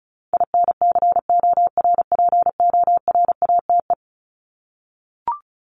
Morse code, heard as a keyed tone, about 34 wpm, 700 Hz. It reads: INCORPORATE
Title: Top 1751-2000 Words 35wpm